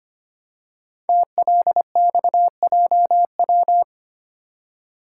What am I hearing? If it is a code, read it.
Morse code, heard as a keyed tone, about 25 wpm, 700 Hz. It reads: TLXJW
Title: Code Groups - 5 Letters - 25wpm